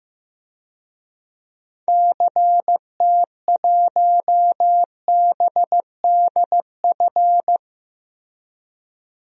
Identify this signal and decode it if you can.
Morse code, heard as a keyed tone, about 16 wpm, 700 Hz. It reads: CT1BDF
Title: DXCC-100 Call Signs - 6 Characters - Repeated 3x - 15wpm